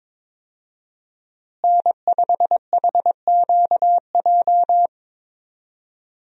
Morse code, heard as a keyed tone, about 22 wpm, 700 Hz. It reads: N5HQJ
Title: USA Call Signs - 5 Characters - Repeated 3x - 22wpm